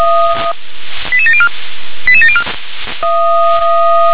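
Five-tone selective call (ZVEI), heard as quick runs of five tones, tones of about 70 ms. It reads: 88704 88704
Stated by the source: ZVEI five-tone selective call received by SDR.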